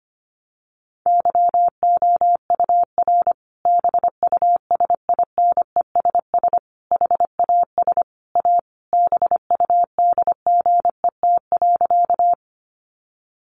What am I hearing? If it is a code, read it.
Morse code, heard as a keyed tone, about 25 wpm, 700 Hz. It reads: YOUR BUSINESS HAS A BUDGET.
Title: Sentences from Top 1200 Words 25wpm